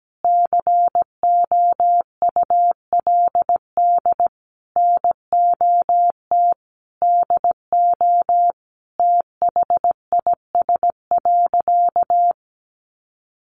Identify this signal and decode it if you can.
Morse code, heard as a keyed tone, about 17 wpm, 700 Hz. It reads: COULD NOT DO THIS.
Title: Sentences from Top 1100 Words 17wpm